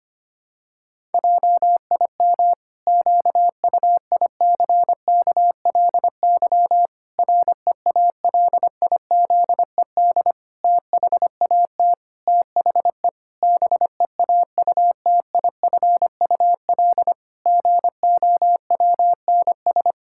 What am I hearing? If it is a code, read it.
Morse code, heard as a keyed tone, about 25 wpm, 700 Hz. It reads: JIM QUICKLY REALIZED THAT THE BEAUTIFUL GOWNS
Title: Pangram Challenge v2 - 25wpm